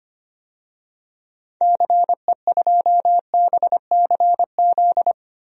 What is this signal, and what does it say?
Morse code, heard as a keyed tone, about 25 wpm, 700 Hz. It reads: CE2BCZ
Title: DXCC-100 Call Signs - 6 Characters - 25wpm